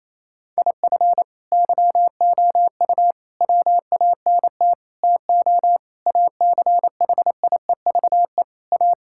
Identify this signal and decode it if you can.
Morse code, heard as a keyed tone, about 28 wpm, 700 Hz. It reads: IF YOU WANT TO ACHIEVE A
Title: Sentences from Top 1000 Words - Review Part 6 - 28wpm